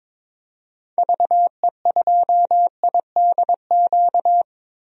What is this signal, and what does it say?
Morse code, heard as a keyed tone, about 22 wpm, 700 Hz. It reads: VE2IDQ
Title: DXCC-100 Call Signs - 6 Characters - Repeated 2x - 22wpm